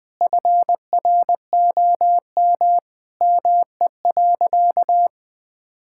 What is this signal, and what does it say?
Morse code, heard as a keyed tone, about 20 wpm, 700 Hz. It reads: FROM ME.
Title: Sentences with 7-Word Phrases - 20wpm